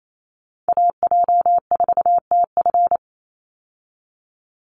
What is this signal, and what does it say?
Morse code, heard as a keyed tone, about 28 wpm, 700 Hz. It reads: AJ4TF
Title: USA Call Signs 28wpm